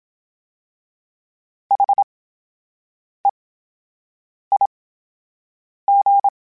Morse code, heard as a keyed tone, about 27 wpm, spaced as for 7 wpm, 800 Hz. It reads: HEIG
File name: Special INGs / Special INGs-mp3s / VerySlowToFaster Ing @25